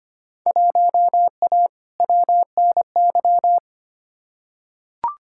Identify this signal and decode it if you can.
Morse code, heard as a keyed tone, about 25 wpm, 700 Hz. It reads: 1A WNY
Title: Field Day Exchanges - 25wpm